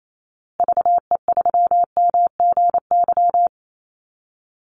Morse code, heard as a keyed tone, about 28 wpm, 700 Hz. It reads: VE3MGY
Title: Call Signs Repeated 3x - 28wpm